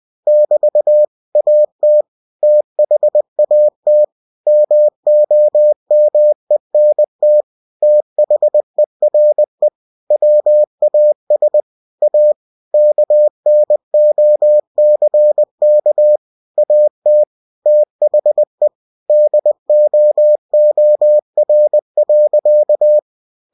Morse code, heard as a keyed tone, about 20 wpm, 600 Hz. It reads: <BT> AT THAT MOMENT THERE WAS A KNOCK AT THE DOOR.
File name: cw-books / winnie-the-pooh / winnie-20wpm-page-0073